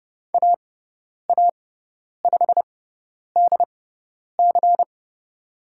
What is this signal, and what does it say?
Morse code, heard as a keyed tone, about 30 wpm, spaced as for 10 wpm, 700 Hz. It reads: AA5DC
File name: Lesson 16g - Letter c - Call Signs 30wpm